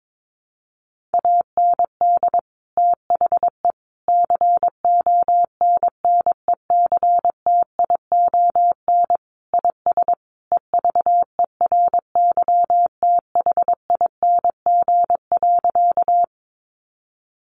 Morse code, heard as a keyed tone, about 22 wpm, 700 Hz. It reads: AND THE CONNECTION IS EVERYTHING.
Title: Sentences from Top 1300 Words 22wpm